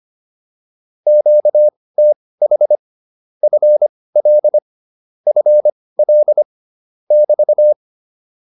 Morse code, heard as a keyed tone, about 25 wpm, spaced as for 18 wpm, 600 Hz. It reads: QTH FL FL <BT>
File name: QSO207 18